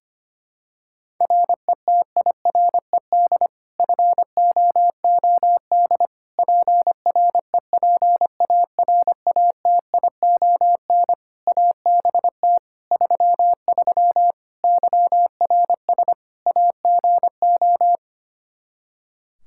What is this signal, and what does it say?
Morse code, heard as a keyed tone, about 25 wpm, 700 Hz. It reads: RETIRED FOOD PREPARATION ABT 33 YRS AGO
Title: K1USN SST QSO Practice 25wpm